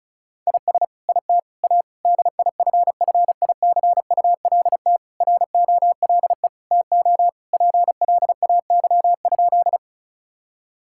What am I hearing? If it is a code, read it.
Morse code, heard as a keyed tone, about 34 wpm, 700 Hz. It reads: IS IT A DIFFICULT ROLE TO PLAY?
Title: Sentences from Top 1000 Words - Review Part 6 - 35wpm